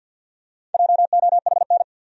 Morse code, heard as a keyed tone, about 52 wpm, 700 Hz. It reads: JOHN
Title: USA Common Names - Max 4 Letters - 50wpm